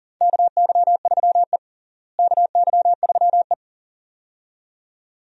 Morse code, heard as a keyed tone, about 40 wpm, 700 Hz. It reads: KY3E KY3E
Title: USA Call Signs - 4 Characters - Repeated 2x - 40wpm